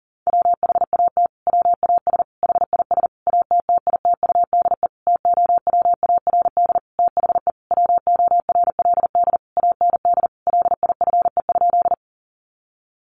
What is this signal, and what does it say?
Morse code, heard as a keyed tone, about 40 wpm, 700 Hz. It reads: WHAT WAS HIS ATTITUDE TOWARD THE WORLD AND LIFE?
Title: Sentences from Top 1300 Words 40wpm